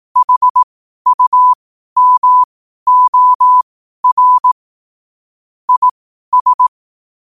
Morse code, heard as a keyed tone, about 18 wpm, spaced as for 12 wpm, 1000 Hz. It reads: HUMOR IS
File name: qotdfiles / 12wpm